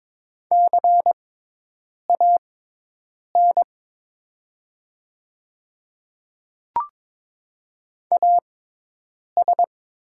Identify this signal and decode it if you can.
Morse code, heard as a keyed tone, about 22 wpm, spaced as for 8 wpm, 700 Hz. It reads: CAN AS
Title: 6x Character Spacing - Top 200 Words - 22wpm